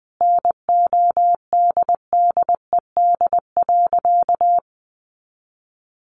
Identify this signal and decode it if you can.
Morse code, heard as a keyed tone, about 20 wpm, 700 Hz. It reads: NODDED.
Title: A Sound Of Thunder 20wpm